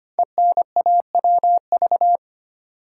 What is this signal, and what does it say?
Morse code, heard as a keyed tone, about 25 wpm, 700 Hz. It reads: ENAWV
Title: Code Groups - 5 Letters - 25wpm